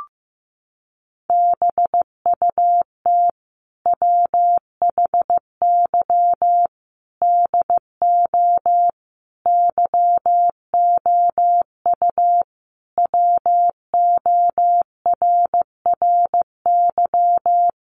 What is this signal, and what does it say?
Morse code, heard as a keyed tone, about 14 wpm, 700 Hz. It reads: BUT WHY DO YOU WORRY
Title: Sentences from Top 1100 Words 15wpm